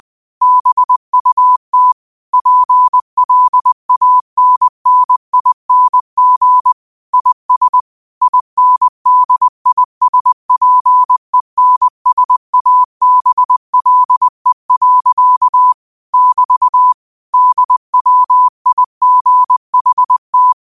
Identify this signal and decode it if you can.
Morse code, heard as a keyed tone, about 20 wpm, 1000 Hz. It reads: BUT PLANNING IS INDISPENSABLE. <BT> DWIGHT
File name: qotdfiles / 20wpm